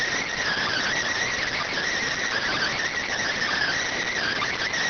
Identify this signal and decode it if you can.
Emergency Alert System (SAME) tones, heard as harsh buzzing bursts. It is heard from 0.4 to 4.4 s.